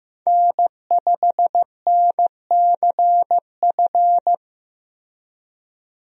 Morse code, heard as a keyed tone, about 15 wpm, 700 Hz. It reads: N5NCF
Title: USA Call Signs - 5 Characters - 15wpm